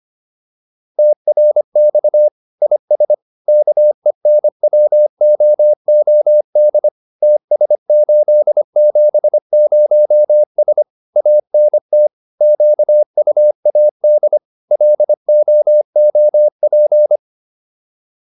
Morse code, heard as a keyed tone, about 25 wpm, 600 Hz. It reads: TRX IS KENWOOD TS870S ANT QUAD LOOP